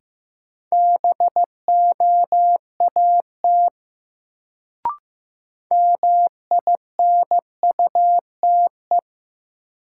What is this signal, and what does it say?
Morse code, heard as a keyed tone, about 15 wpm, 700 Hz. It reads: BOAT MINUTE
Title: Sentences from Top 500 Words - New Word Drill - 15wpm